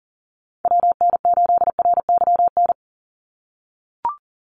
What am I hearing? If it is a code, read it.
Morse code, heard as a keyed tone, about 38 wpm, 700 Hz. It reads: WN8RYN
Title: USA Call Signs 40wpm